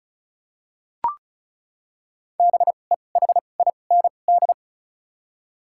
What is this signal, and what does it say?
Morse code, heard as a keyed tone, about 36 wpm, spaced as for 25 wpm, 700 Hz. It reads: BEHIND
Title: 2x Character Spacing - Top 500 Words - 35wpm